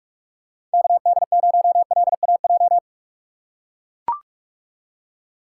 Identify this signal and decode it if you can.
Morse code, heard as a keyed tone, about 43 wpm, 700 Hz. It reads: KD0LAJ
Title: USA Call Signs - 6 Characters - 4x Character Spacing - 45wpm